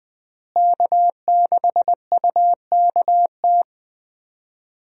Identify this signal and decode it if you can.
Morse code, heard as a keyed tone, about 21 wpm, 700 Hz. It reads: K6UKT
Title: USA Call Signs - 5 Characters - Repeated 3x - 20wpm